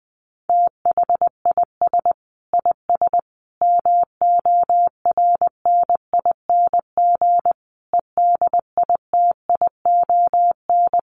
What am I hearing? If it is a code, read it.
Morse code, heard as a keyed tone, about 20 wpm, 700 Hz. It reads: THIS IS MORNING EDITION
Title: Sets of 5 Words Encore2 20wpm